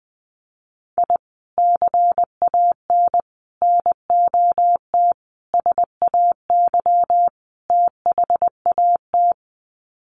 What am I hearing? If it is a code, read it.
Morse code, heard as a keyed tone, about 19 wpm, 700 Hz. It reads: I CAN NOT SAY THAT
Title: Sets of 5 Words Encore2 20wpm